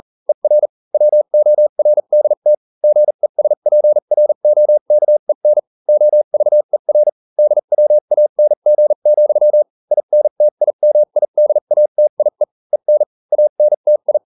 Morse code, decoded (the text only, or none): ER WORDT GESPROKEN OVER DWANG, INTIMIDATIE EN ANTI